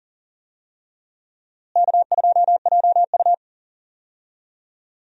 Morse code, heard as a keyed tone, about 40 wpm, 700 Hz. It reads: K2JU
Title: USA Call Signs - 4 Characters - Repeated 2x - 40wpm